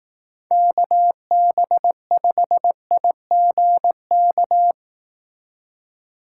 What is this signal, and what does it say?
Morse code, heard as a keyed tone, about 18 wpm, 700 Hz. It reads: KB5IGK
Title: Speed Racing - USA 2x3 Call Signs 15wpm